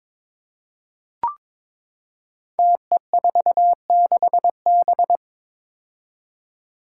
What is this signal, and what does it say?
Morse code, heard as a keyed tone, about 21 wpm, 700 Hz. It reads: TE46B
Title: Five Challenging Letter-Number Combinations 22wpm